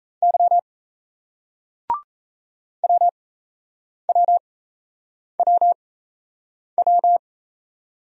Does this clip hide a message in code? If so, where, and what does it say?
Yes: Morse code from 0.2 to 7.2 s, heard as a keyed tone, about 36 wpm, spaced as for 8 wpm, 700 Hz. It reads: Y WWWW